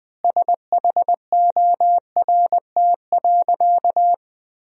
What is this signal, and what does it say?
Morse code, heard as a keyed tone, about 21 wpm, 700 Hz. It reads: SHORT.
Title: 2x Word-Spacing - Sentences from Top 500 Words - 20wpm